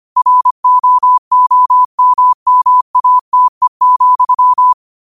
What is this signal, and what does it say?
Morse code, heard as a keyed tone, about 25 wpm, 1000 Hz. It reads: ROOMMATE,